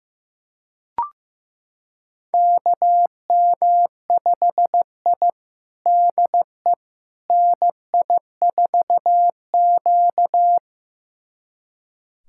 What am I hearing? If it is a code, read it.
Morse code, heard as a keyed tone, about 15 wpm, 700 Hz. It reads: KM5I DE NI4Q
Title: USA Callsign DE Callsign - 4 Characters - 15wpm